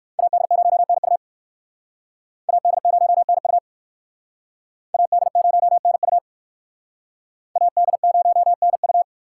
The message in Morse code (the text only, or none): AD0NU AD0NU AD0NU AD0NU